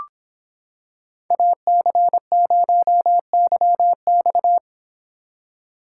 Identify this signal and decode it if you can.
Morse code, heard as a keyed tone, about 26 wpm, 700 Hz. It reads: AC0YX